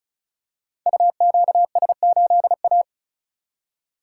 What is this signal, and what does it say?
Morse code, heard as a keyed tone, about 36 wpm, 700 Hz. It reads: UQS8A
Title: Code Groups - 5 Letter:Numbers - 35wpm